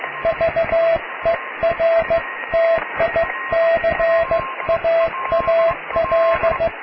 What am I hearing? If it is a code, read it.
Morse code, heard as a keyed tone, about 15 wpm, 660 Hz, with radio noise. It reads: VERTICAAL
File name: cw / tekst(a)+QRM